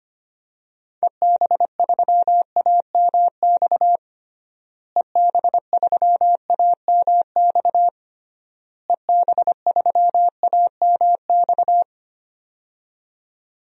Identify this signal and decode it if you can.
Morse code, heard as a keyed tone, about 25 wpm, 700 Hz. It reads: EB3AMX EB3AMX EB3AMX
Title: DXCC-100 Call Signs - 6 Characters - Repeated 3x - 25wpm